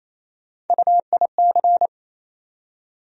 Morse code, heard as a keyed tone, about 28 wpm, 700 Hz. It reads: UIC
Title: USA Call Sign Postfixes - 3 Letters 28wpm